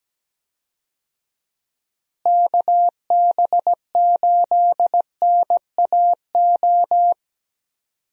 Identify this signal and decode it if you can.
Morse code, heard as a keyed tone, about 17 wpm, 700 Hz. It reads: KB8NAO
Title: USA Call Signs - 6 Characters - Repeated 2x - 17wpm